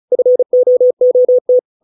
Morse code, heard as a keyed tone, about 34 wpm, 500 Hz. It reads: FOOT